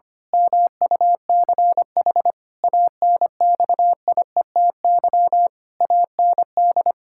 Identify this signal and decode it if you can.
Morse code, heard as a keyed tone, about 25 wpm, 700 Hz. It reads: MUCH ANXIETY AND
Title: Pangram Challenge v2 - 25wpm